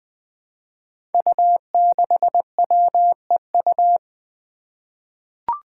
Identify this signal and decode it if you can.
Morse code, heard as a keyed tone, about 20 wpm, 700 Hz. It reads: U6WEU
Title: Five Challenging Letter-Number Combinations 20wpm